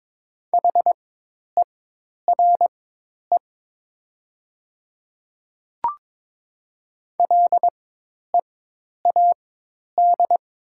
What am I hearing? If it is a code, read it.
Morse code, heard as a keyed tone, about 21 wpm, spaced as for 10 wpm, 700 Hz. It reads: HERE LEAD
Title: 4x Character Spacing - Top 500 Words - 22wpm